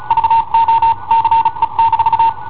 Morse code, heard as a keyed tone, about 36 wpm, spaced as for 30 wpm, 910 Hz. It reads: VOCE-
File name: i7ohp / cassetta i7ohp B